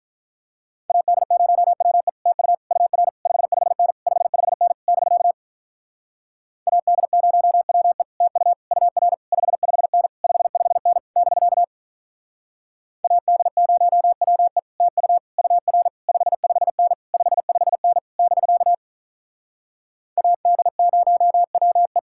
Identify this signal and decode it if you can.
Morse code, heard as a keyed tone, about 46 wpm, 700 Hz. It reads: AD0WE TU UR 55N 55N <BK> AD0WE TU UR 55N 55N <BK> AD0WE TU UR 55N 55N <BK> AD0WE